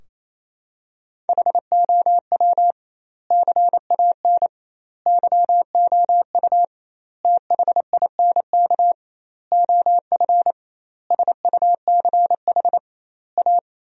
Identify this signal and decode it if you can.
Morse code, heard as a keyed tone, about 27 wpm, 700 Hz. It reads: HOW CAN YOU THINK OF SUCH A
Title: 2x Word-Spacing - Sentences from Top 200 Words - 28wpm